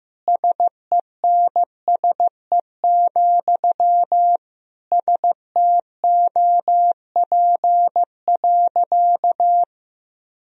Morse code, heard as a keyed tone, about 16 wpm, 700 Hz. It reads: SENSE, STOP.
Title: Sentences from Top 1000 Words - Review Part 4 - 15wpm